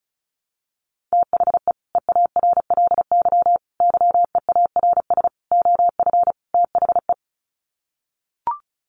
Morse code, heard as a keyed tone, about 34 wpm, 700 Hz. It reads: THE EARLY YEARS OF THE
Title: Sets of 5 Words Encore 35wpm